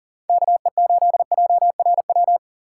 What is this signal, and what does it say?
Morse code, heard as a keyed tone, about 40 wpm, 700 Hz. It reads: KE8JRW
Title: USA Call Signs - 6 Characters - 6x Character Spacing - 40wpm